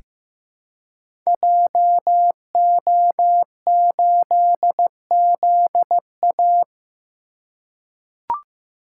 Morse code, heard as a keyed tone, about 16 wpm, 700 Hz. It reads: JO8ZA
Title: Code Groups - 5 Letter:Numbers - 15wpm